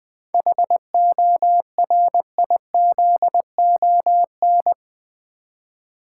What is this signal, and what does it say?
Morse code, heard as a keyed tone, about 20 wpm, 700 Hz. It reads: HORIZON
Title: 7 Letter Words - First 3 Letters Given - 20wpm